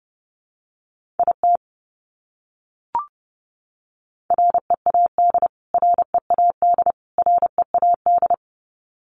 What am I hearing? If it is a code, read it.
Morse code, heard as a keyed tone, about 30 wpm, 700 Hz. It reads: IT READ READ READ
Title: Top 200 Words Repeated 30wpm